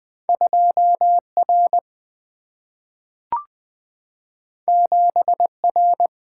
Morse code, heard as a keyed tone, about 20 wpm, 700 Hz. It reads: 2 R 7 R
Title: Single-Letter Single-Number 20wpm